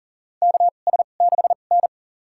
Morse code, heard as a keyed tone, about 40 wpm, spaced as for 29 wpm, 700 Hz. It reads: KS6N
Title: USA Call Signs - 4 Characters - 2x Character Spacing - 40wpm